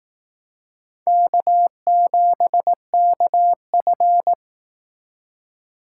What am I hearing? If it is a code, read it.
Morse code, heard as a keyed tone, about 18 wpm, 700 Hz. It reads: K7KF
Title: Speed Racing - USA 1x2 Call Signs 15wpm